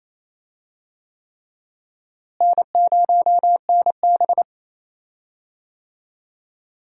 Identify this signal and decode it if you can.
Morse code, heard as a keyed tone, about 28 wpm, 700 Hz. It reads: N0NB
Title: USA Call Signs - 4 Characters - Repeated 3x - 28wpm